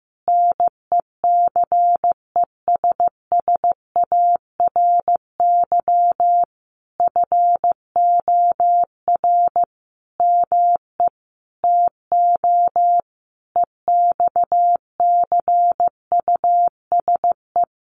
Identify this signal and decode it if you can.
Morse code, heard as a keyed tone, about 15 wpm, 700 Hz. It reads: NECESSARY FOR ME TO EXCUSE
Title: Sentences from Top 1300 Words 15wpm